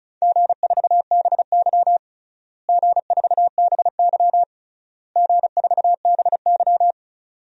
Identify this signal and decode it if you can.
Morse code, heard as a keyed tone, about 36 wpm, 700 Hz. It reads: G4BY G4BY G4BY
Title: DXCC-100 Call Signs - 4 Characters - Repeated 3x - 35wpm